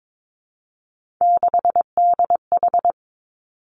Morse code, heard as a keyed tone, about 22 wpm, 700 Hz. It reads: 6DH
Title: Three Challenging Letter-Number Combinations 22wpm